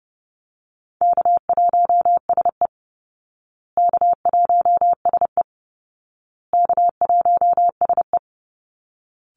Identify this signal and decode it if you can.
Morse code, heard as a keyed tone, about 30 wpm, 700 Hz. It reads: K1SE K1SE K1SE
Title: Easy Call Signs Repeated 3x - 30wpm